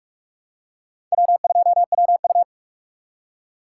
Morse code, heard as a keyed tone, about 43 wpm, 700 Hz. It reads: W2WU
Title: USA Call Signs - 2x Character Spacing - 45wpm